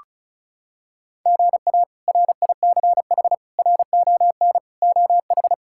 Morse code, heard as a keyed tone, about 36 wpm, 700 Hz. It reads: GA RICH RON OH
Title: K1USN SST QSO Practice - 35wpm